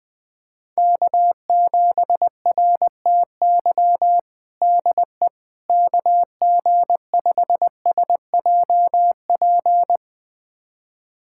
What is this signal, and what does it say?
Morse code, heard as a keyed tone, about 20 wpm, 700 Hz. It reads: K7RTY DE KG5SJP
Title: USA Callsign DE Callsign 20wpm